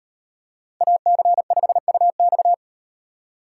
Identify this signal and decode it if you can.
Morse code, heard as a keyed tone, about 38 wpm, 700 Hz. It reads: AC5UX